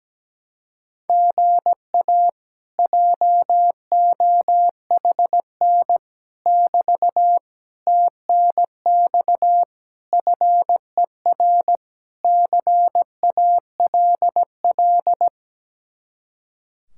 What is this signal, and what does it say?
Morse code, heard as a keyed tone, about 17 wpm, 700 Hz. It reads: GA JOHN <BT> TNX FER CALL
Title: Speed-Racing Common QSO Phrases - 17wpm